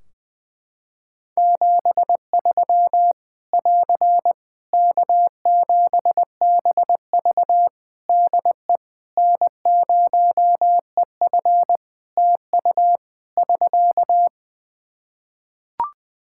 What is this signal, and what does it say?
Morse code, heard as a keyed tone, about 20 wpm, 700 Hz. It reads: 73 <AR> K7BV DE N0EF TU <SK>
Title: Practice QSO - Exchange 4 - Ending - 20wpm